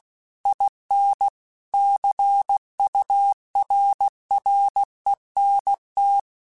Morse code, heard as a keyed tone, about 16 wpm, 800 Hz. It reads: IN CURRENT
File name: Technical Subjects / Technical Subjects-mp3s / Reactance @15